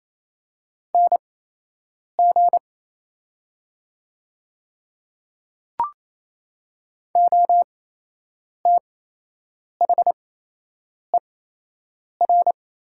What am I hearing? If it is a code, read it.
Morse code, heard as a keyed tone, about 27 wpm, spaced as for 8 wpm, 700 Hz. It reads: NG OTHER
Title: 8x Character Spacing - Top 100 Words - 28wpm